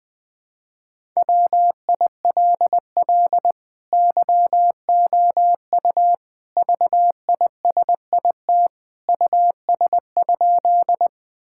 Morse code, heard as a keyed tone, about 20 wpm, 700 Hz. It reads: WILL YOU VISIT US?
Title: Sentences from Top 1000 Words - Review Part 6 - 20wpm